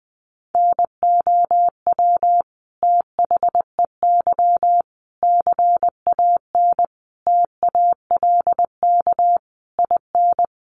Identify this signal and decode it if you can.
Morse code, heard as a keyed tone, about 20 wpm, 700 Hz. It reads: NOW THEY CAN TALK IN
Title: Sentences from Top 700 Words 20wpm